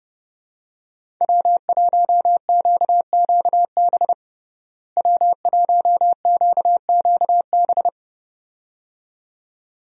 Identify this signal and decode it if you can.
Morse code, heard as a keyed tone, about 30 wpm, 700 Hz. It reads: W1QQB W1QQB
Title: USA Call Signs - 5 Characters - Repeated 2x - 30wpm